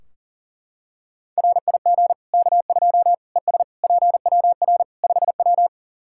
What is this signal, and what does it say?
Morse code, heard as a keyed tone, about 37 wpm, 700 Hz. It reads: RIG K2 ES PWR 5W
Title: Practice QSO - Exchange 2 - Info - 40wpm